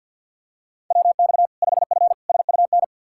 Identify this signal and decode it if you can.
Morse code, heard as a keyed tone, about 47 wpm, 700 Hz. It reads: WX 5F SUN
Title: Practice QSO - Exchange 2 - Info - 50wpm